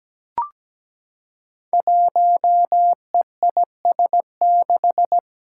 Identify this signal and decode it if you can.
Morse code, heard as a keyed tone, about 17 wpm, 700 Hz. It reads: 1EIS6
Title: Five Challenging Letter-Number Combinations 17wpm